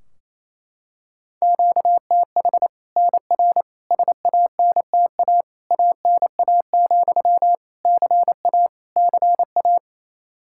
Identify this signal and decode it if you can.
Morse code, heard as a keyed tone, about 27 wpm, 700 Hz. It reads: QTH NR SANTA ANA, CA CA
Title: Practice QSO - Exchange 1 - Beginning - 28wpm